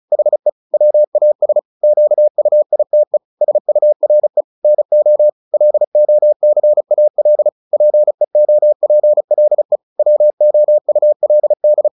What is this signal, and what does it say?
Morse code, heard as a keyed tone, about 34 wpm, 600 Hz. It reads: HE WAS QUITE SURE NO LOCAL PEOPLE WOULD
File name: the old man and the sea / the old man and the sea 1 35wpm